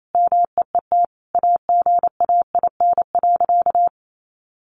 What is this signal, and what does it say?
Morse code, heard as a keyed tone, about 28 wpm, 700 Hz. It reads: MEET AGAIN.
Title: Sentences from Top 1100 Words 28wpm